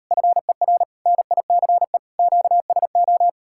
Morse code, heard as a keyed tone, about 38 wpm, 700 Hz. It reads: FER NICE QSO